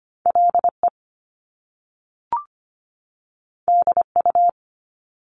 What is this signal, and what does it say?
Morse code, heard as a keyed tone, about 25 wpm, 700 Hz. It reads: LE DU
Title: Top Two Letter Combinations in Words 25wpm